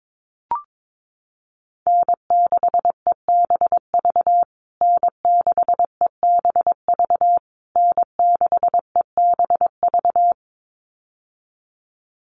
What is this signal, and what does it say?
Morse code, heard as a keyed tone, about 22 wpm, 700 Hz. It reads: N6EBV N6EBV N6EBV
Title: USA Call Signs Repeated 3x 22wpm